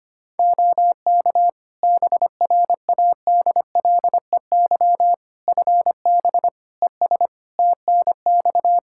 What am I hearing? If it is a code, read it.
Morse code, heard as a keyed tone, about 25 wpm, 700 Hz. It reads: OK BRADLEY FB ES TNX
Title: Practice QSO - Exchange 2 - Info - 25wpm